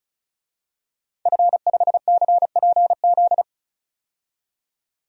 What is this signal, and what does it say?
Morse code, heard as a keyed tone, about 36 wpm, 700 Hz. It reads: F5CPZ
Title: DXCC-100 Call Signs - 5 Characters - 35wpm